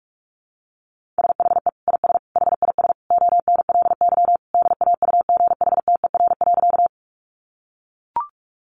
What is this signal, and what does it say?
Morse code, heard as a keyed tone, about 44 wpm, 700 Hz. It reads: SHE IS HIS ONLY DAUGHTER.
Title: Sentences from Top 1500 Words 45wpm